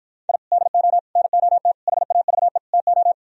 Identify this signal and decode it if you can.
Morse code, heard as a keyed tone, about 55 wpm, 700 Hz. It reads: I DO NOT HAVE TO